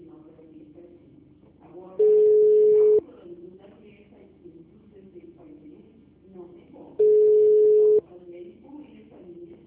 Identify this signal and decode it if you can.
Ringback tone (Europe, 425 Hz), with speech.